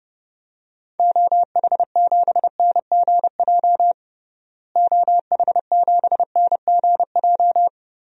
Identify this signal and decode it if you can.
Morse code, heard as a keyed tone, about 31 wpm, 700 Hz. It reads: OH7NGJ OH7NGJ
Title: DXCC-100 Call Signs - 6 Characters - Repeated 3x - 30wpm